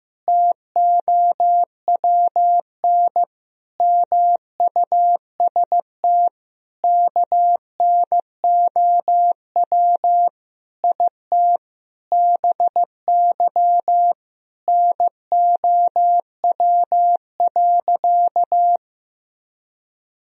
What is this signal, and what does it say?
Morse code, heard as a keyed tone, about 15 wpm, 700 Hz. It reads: TOWN MUST KNOW IT BY NOW.
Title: Sentences from Top 1000 Words - Review Part 2 - 15wpm